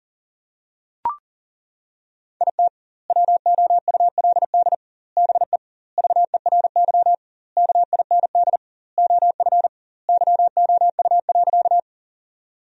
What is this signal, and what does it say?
Morse code, heard as a keyed tone, about 40 wpm, 700 Hz. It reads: IT WOULD BE VERY KIND OF YOU.
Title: 2x Word-Spacing - Sentences from Top 200 Words - 40wpm